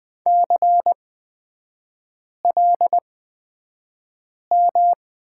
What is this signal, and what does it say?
Morse code, heard as a keyed tone, about 20 wpm, spaced as for 5 wpm, 700 Hz. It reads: CLM
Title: 255 Lesson 41e - Period - Three Characters 20wpm